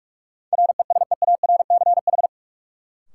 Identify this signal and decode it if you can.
Morse code, heard as a keyed tone, about 46 wpm, 700 Hz. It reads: RESEARCH
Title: Sentences from Top 1000 Words - Review Part 4 - New Word Drill - 45wpm